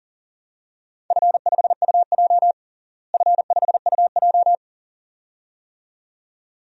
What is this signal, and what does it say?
Morse code, heard as a keyed tone, about 37 wpm, 690 Hz. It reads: F5UJ F5UJ
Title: DXCC-100 Call Signs - 4 Characters - Repeated 2x - 40wpm